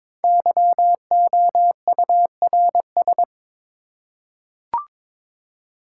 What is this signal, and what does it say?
Morse code, heard as a keyed tone, about 22 wpm, 700 Hz. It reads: YOURS
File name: 5 Letter Words - First 3 Letters Given - 22wpm